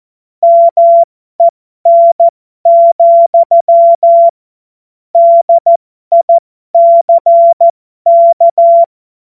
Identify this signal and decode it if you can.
Morse code, heard as a keyed tone, about 14 wpm, spaced as for 12 wpm, 680 Hz. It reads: MEN, DICK
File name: Morsekurs / Witze